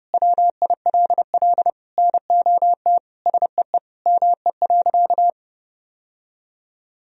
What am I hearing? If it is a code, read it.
Morse code, heard as a keyed tone, about 30 wpm, 700 Hz. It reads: WILL NOT SEE ME.
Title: Sentences from Top 1000 Words - Review Part 1 - 30wpm